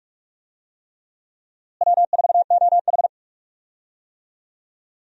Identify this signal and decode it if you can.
Morse code, heard as a keyed tone, about 47 wpm, 700 Hz. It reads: W4OH
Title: USA Call Signs - 4 Characters - Repeated 3x - 45wpm